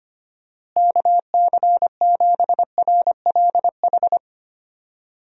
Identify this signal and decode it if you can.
Morse code, heard as a keyed tone, about 25 wpm, 700 Hz. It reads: KC7RLH
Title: USA Call Signs - 6 Characters - Repeated 3x - 25wpm